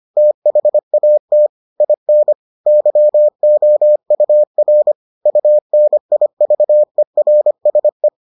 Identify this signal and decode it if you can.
Morse code, heard as a keyed tone, about 25 wpm, 600 Hz. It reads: THAT IN YOUR UNIVERSE